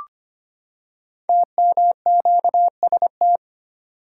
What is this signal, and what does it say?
Morse code, heard as a keyed tone, about 25 wpm, 700 Hz. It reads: TMQST
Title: Code Groups - 5 Letter:Numbers - 25wpm